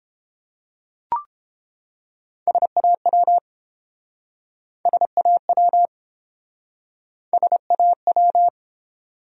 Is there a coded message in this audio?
Yes: Morse code from 2.5 to 8.5 s, heard as a keyed tone, about 29 wpm, 700 Hz. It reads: SAW SAW SAW